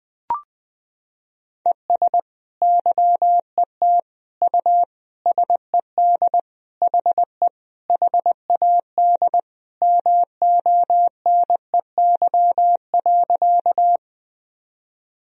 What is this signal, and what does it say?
Morse code, heard as a keyed tone, about 20 wpm, 700 Hz. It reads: ES YET U SED HE HAD MONEY.
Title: Sentences from Top 500 Words and Abbreviations 20wpm